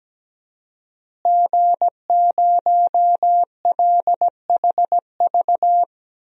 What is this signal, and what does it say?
Morse code, heard as a keyed tone, about 17 wpm, 700 Hz. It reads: G0LHV
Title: DXCC-100 Call Signs - 5 Characters - Repeated 3x - 17wpm